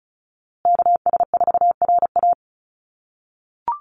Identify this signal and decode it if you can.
Morse code, heard as a keyed tone, about 34 wpm, 700 Hz. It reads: KS4RA
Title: USA Call Signs Repeated 2x 35wpm